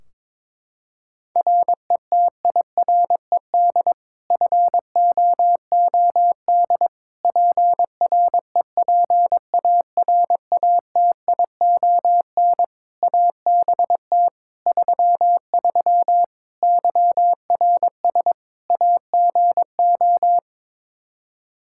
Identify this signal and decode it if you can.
Morse code, heard as a keyed tone, about 21 wpm, 700 Hz. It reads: RETIRED FOOD PREPARATION ABT 33 YRS AGO
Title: K1USN SST QSO Practice 22wpm